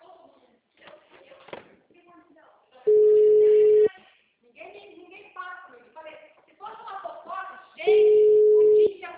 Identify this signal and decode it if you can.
Ringback tone (Europe, 425 Hz), with speech.